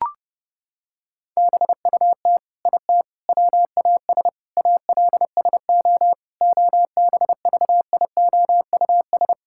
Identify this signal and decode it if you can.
Morse code, heard as a keyed tone, about 31 wpm, 700 Hz. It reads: BUT IT WAS ALSO OBVIOUS
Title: Sentences from Top 1600 Words 30wpm